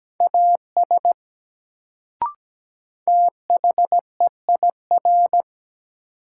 Morse code, heard as a keyed tone, about 17 wpm, 700 Hz. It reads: AS THEIR
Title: Sets of 2 Words - New Word Drill - 17wpm